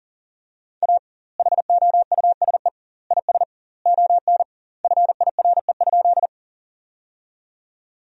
Morse code, heard as a keyed tone, about 40 wpm, 700 Hz. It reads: A HOUSE IS ON FIRE?
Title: 2x Word-Spacing - Sentences from Top 400 Words - 40wpm